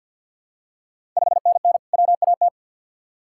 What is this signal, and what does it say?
Morse code, heard as a keyed tone, about 46 wpm, 690 Hz. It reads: 5NN WAT
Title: Ontario QSO Party Exchange 50wpm